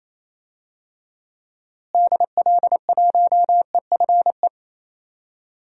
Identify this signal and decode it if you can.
Morse code, heard as a keyed tone, about 28 wpm, 700 Hz. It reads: DL1EFE
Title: DXCC-100 Call Signs - 6 Characters - Repeated 2x - 28wpm